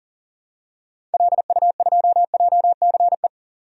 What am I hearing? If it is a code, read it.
Morse code, heard as a keyed tone, about 37 wpm, 700 Hz. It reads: LU2JCE
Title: DXCC-100 Call Signs - 6 Characters - 40wpm